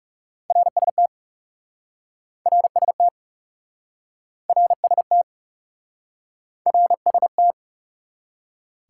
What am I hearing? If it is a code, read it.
Morse code, heard as a keyed tone, about 40 wpm, 700 Hz. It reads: RST RST RST RST